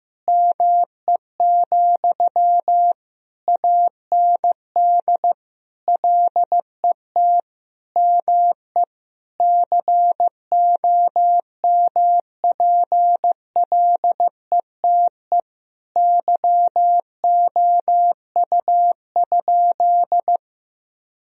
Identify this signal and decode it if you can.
Morse code, heard as a keyed tone, about 15 wpm, 700 Hz. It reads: ME, AND LET ME COMPLETE YOU?
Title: Sentences from Top 1000 Words - Review Part 2 - 15wpm